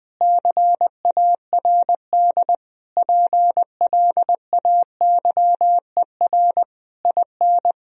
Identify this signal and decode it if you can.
Morse code, heard as a keyed tone, about 20 wpm, 700 Hz. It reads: CARD PLAYER IN
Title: Sentences from Top 1000 Words - Review Part 4 - 20wpm